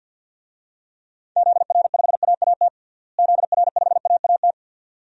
Morse code, heard as a keyed tone, about 51 wpm, 700 Hz. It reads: ZR5AAT ZR5AAT
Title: DXCC-100 Call Signs - 6 Characters - Repeated 2x - 50wpm